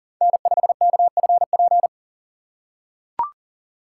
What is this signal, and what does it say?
Morse code, heard as a keyed tone, about 39 wpm, 700 Hz. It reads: N5KFP
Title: USA Call Signs - 5 Characters - 3x Character Spacing - 40wpm